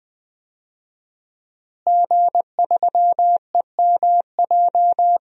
Morse code, heard as a keyed tone, about 20 wpm, 700 Hz. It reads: G3EMJ
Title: DXCC-100 Call Signs - 5 Characters - Repeated 3x - 20wpm